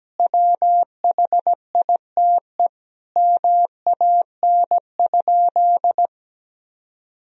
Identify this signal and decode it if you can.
Morse code, heard as a keyed tone, about 17 wpm, 700 Hz. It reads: WHITE MAN?
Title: Sentences from Top 1000 Words - Review Part 1 - 17wpm